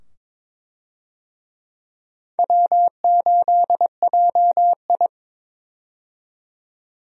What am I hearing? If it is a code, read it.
Morse code, heard as a keyed tone, about 22 wpm, 700 Hz. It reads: W8JI
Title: USA Call Signs - 4 Characters - 22wpm